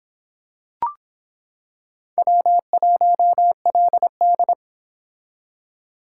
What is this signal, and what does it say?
Morse code, heard as a keyed tone, about 26 wpm, 700 Hz. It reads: W1LD